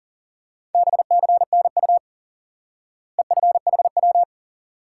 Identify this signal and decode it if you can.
Morse code, heard as a keyed tone, about 40 wpm, 700 Hz. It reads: BCNU EFHW
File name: IQE Course 3 - Lesson 89 - YAGI - Two Elements - 40wpm